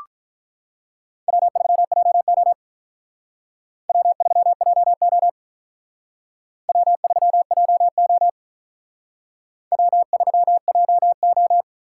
Morse code, heard as a keyed tone, about 44 wpm, 700 Hz. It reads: W3JO W3JO W3JO W3JO